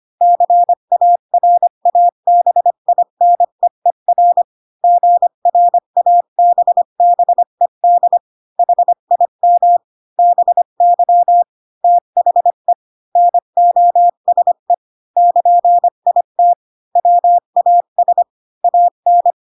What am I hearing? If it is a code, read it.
Morse code, heard as a keyed tone, about 25 wpm, 700 Hz. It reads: CARABINEER GRABBED HIM BY THE NOSE <KN>IT WAS AN
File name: Pinocchio / Pinocchio ch3 20 25 ik1hge